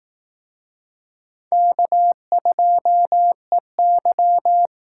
Morse code, heard as a keyed tone, about 18 wpm, 700 Hz. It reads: K2EY
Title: Speed Racing - USA 1x2 Call Signs 15wpm